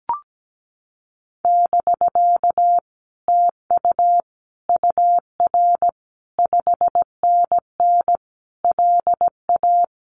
Morse code, heard as a keyed tone, about 17 wpm, 700 Hz. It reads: <BK> TU UR 5NN LA
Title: Common QSO Phrases Encore 17wpm